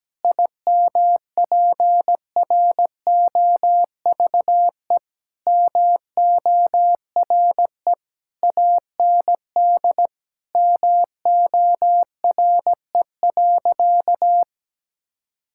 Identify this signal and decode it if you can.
Morse code, heard as a keyed tone, about 17 wpm, 700 Hz. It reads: IMPROVE MORE AND MORE.
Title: Sentences from Top 1000 Words - Review Part 5 - 17wpm